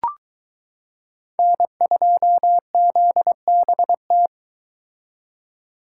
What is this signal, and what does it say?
Morse code, heard as a keyed tone, about 23 wpm, 700 Hz. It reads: N2ZBT